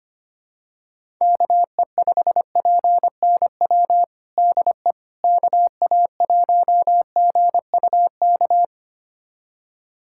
Morse code, heard as a keyed tone, about 25 wpm, 700 Hz. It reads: KE5PNW DE KA1GUK
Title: USA Callsign DE Callsign - 6 Characters - 25wpm